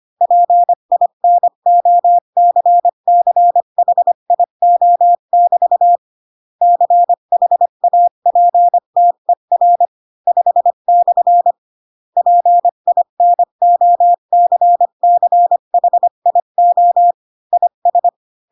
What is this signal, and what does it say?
Morse code, heard as a keyed tone, about 25 wpm, 700 Hz. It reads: PINOCCHIO<BT> CHAPTER 5/ PINOCCHIO IS
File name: Pinocchio / Pinocchio ch5 20 25 ik1hge